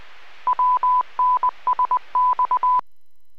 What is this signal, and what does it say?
Morse code, heard as a keyed tone, about 20 wpm, 1000 Hz. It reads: WNSX